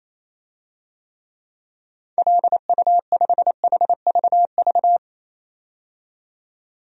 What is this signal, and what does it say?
Morse code, heard as a keyed tone, about 28 wpm, 700 Hz. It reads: LU5HVV
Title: DXCC-100 Call Signs - 6 Characters - 28wpm